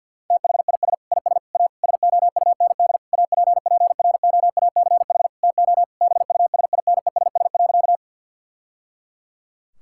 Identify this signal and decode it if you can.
Morse code, heard as a keyed tone, about 50 wpm, 700 Hz. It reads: THIS IS A SOUND APPROACH TO BUSINESS.
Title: Sentences from Top 1000 Words - Review Part 6 - 50wpm